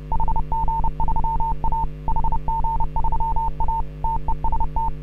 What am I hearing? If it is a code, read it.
Morse code, heard as a keyed tone, about 30 wpm, 900 Hz. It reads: HG3A HG3A TEST